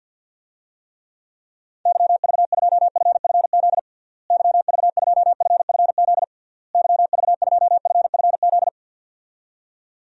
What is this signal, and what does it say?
Morse code, heard as a keyed tone, about 52 wpm, 700 Hz. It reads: YV2FFZ YV2FFZ YV2FFZ
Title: DXCC-100 Call Signs - 6 Characters - Repeated 3x - 50wpm